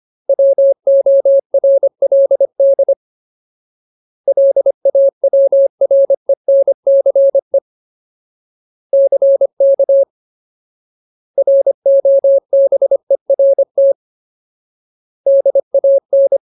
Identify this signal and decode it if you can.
Morse code, heard as a keyed tone, about 25 wpm, 550 Hz. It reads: WORLD LAWRENCE CK ROBERT DAN
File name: Ultimate Word List / 25WPM Ultimate